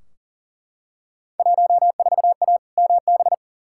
Morse code, heard as a keyed tone, about 40 wpm, 700 Hz. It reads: 14A MB
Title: Field Day Exchanges Encore - 40wpm